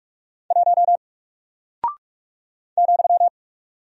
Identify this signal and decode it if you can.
Morse code, heard as a keyed tone, about 46 wpm, 700 Hz. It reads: 1 ,
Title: Single Character (Letter:Number:Punctuation:Prosign) 45wpm